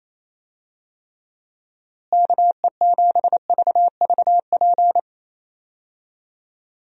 Morse code, heard as a keyed tone, about 28 wpm, 700 Hz. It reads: KE7VVP
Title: USA Call Signs - 6 Characters - Repeated 3x - 28wpm